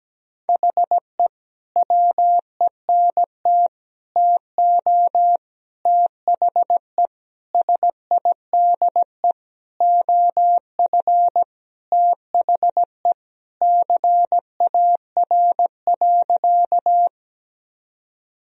Morse code, heard as a keyed tone, about 17 wpm, 700 Hz. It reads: HE WENT TO THE SIDE OF THE CAR.
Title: Sentences from Top 1000 Words - Review Part 1 - 17wpm